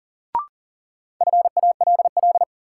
Morse code, heard as a keyed tone, about 40 wpm, 700 Hz. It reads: FALL
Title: Sentences from Top 500 Words - New Word Drill - 40wpm